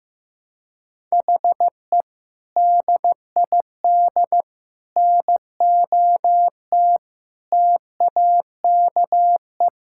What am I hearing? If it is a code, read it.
Morse code, heard as a keyed tone, about 15 wpm, 700 Hz. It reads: HE DID NOT TAKE
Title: Sentences from Top 1000 Words - Review Part 6 - 15wpm